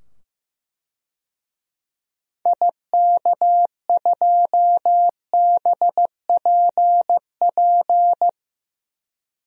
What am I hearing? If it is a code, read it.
Morse code, heard as a keyed tone, about 15 wpm, 700 Hz. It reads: IK2BPP
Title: DXCC-100 Call Signs - 6 Characters - 15wpm